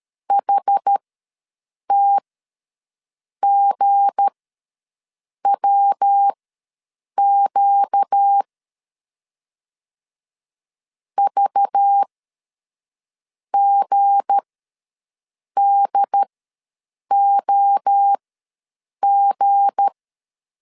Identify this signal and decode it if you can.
Morse code, heard as a keyed tone, about 13 wpm, spaced as for 6 wpm, 800 Hz. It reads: HTGWQ VGDOG
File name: LEZIONI / 7 GF9Z / 7 G ws 12wpm 3140